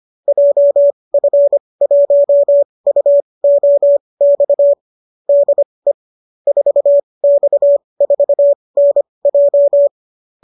Morse code, heard as a keyed tone, about 25 wpm, spaced as for 20 wpm, 570 Hz. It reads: JF1UOX DE 4X4NJ